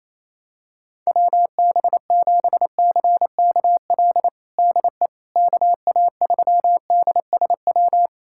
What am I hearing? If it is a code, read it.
Morse code, heard as a keyed tone, about 28 wpm, 700 Hz. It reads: WB7CKL DE KA3DSW
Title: USA Callsign DE Callsign - 6 Characters - 28wpm